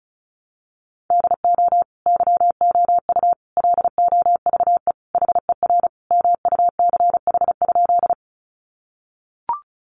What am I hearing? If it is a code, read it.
Morse code, heard as a keyed tone, about 34 wpm, 700 Hz. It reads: DO YOU LOVE HER MUCH?
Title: Sentences from Top 400 Words 35wpm